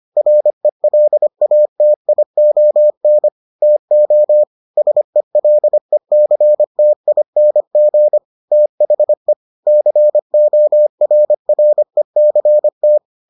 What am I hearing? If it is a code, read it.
Morse code, heard as a keyed tone, about 25 wpm, 600 Hz. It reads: RELATION TO SELECTING THE CORRECT